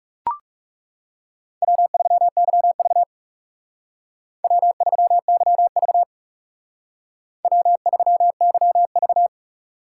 Morse code, heard as a keyed tone, about 39 wpm, 700 Hz. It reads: W3YV W3YV W3YV